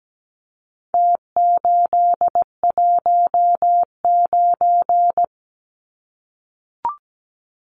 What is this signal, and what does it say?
Morse code, heard as a keyed tone, about 17 wpm, 700 Hz. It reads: T819
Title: Numbers - Contest Style - 4-Digit 17wpm